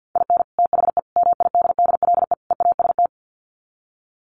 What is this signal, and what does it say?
Morse code, heard as a keyed tone, about 50 wpm, 700 Hz. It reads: IN THE MIDDLE EAST
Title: Sets of 4 Words 50wpm